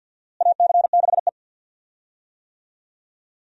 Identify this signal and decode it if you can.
Morse code, heard as a keyed tone, about 52 wpm, 690 Hz. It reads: AC6E
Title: USA Call Signs - 4 Characters - Repeated 2x - 50wpm